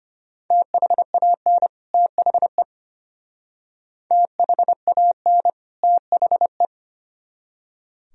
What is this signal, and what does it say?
Morse code, heard as a keyed tone, about 32 wpm, 700 Hz. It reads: THAN THE THAN THE